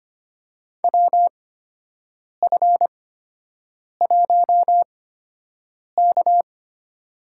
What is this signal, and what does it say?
Morse code, heard as a keyed tone, about 25 wpm, spaced as for 7 wpm, 700 Hz. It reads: WF1K
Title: USA Call Signs - 8x Character Spacing - 25wpm